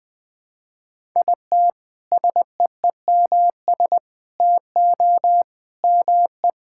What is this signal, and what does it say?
Morse code, heard as a keyed tone, about 20 wpm, 700 Hz. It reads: IT SEEMS TO ME
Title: Sentences with 5-Word Phrases - 20wpm